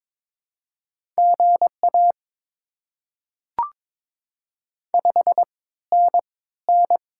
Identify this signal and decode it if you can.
Morse code, heard as a keyed tone, about 22 wpm, spaced as for 15 wpm, 700 Hz. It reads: GA 5NN
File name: Top 20 QSO Elements - 3x Character Spacing - 22wpm